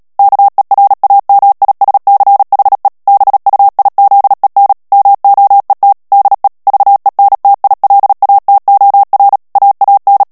Morse code, heard as a keyed tone, about 37 wpm, 800 Hz. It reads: KERAMISCHE BUIZEN MOET DE VENTILATOR AAN
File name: morse mp3 / HOLLANDAIS texte clair 2 vitesse 35 mots minute